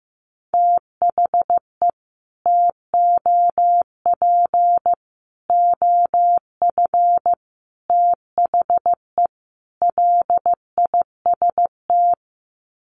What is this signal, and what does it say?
Morse code, heard as a keyed tone, about 15 wpm, 700 Hz. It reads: THE TOP OF THE LIST
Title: Sets of 5 Words Encore2 15wpm